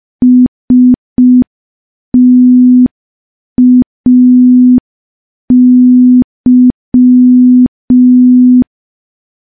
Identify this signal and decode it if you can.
Morse code, heard as a keyed tone, about 5 wpm, 250 Hz. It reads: STAY
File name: morse / morse-training-05WPM